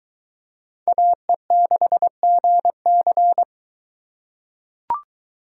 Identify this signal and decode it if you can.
Morse code, heard as a keyed tone, about 22 wpm, 700 Hz. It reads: AE6GC